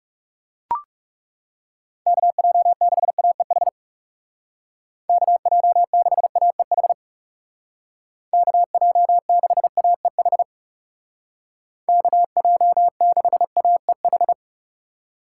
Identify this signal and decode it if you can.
Morse code, heard as a keyed tone, about 36 wpm, 700 Hz. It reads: KJ6AEH KJ6AEH KJ6AEH KJ6AEH
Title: Speed Racing - USA 2x3 Call Signs 30wpm